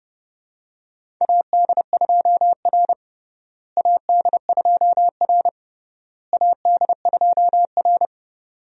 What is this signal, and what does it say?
Morse code, heard as a keyed tone, about 30 wpm, 700 Hz. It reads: AD2R AD2R AD2R
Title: USA Call Signs - 4 Characters - Repeated 3x - 30wpm